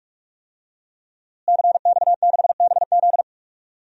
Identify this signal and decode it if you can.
Morse code, heard as a keyed tone, about 42 wpm, 700 Hz. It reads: CX6BZ
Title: DXCC-100 Call Signs - 5 Characters - Repeated 2x - 45wpm